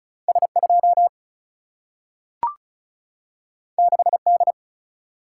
Morse code, heard as a keyed tone, about 36 wpm, 700 Hz. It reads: S2 6D
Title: Single-Letter Single-Number 35wpm